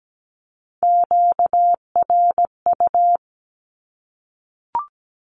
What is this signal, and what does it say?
Morse code, heard as a keyed tone, about 17 wpm, 700 Hz. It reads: QRU
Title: Q-Codes Repeated 3x 17wpm